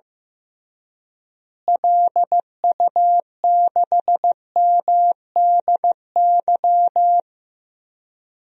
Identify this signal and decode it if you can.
Morse code, heard as a keyed tone, about 15 wpm, 700 Hz. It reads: LU6MDY
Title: DXCC-100 Call Signs - 6 Characters - Repeated 3x - 15wpm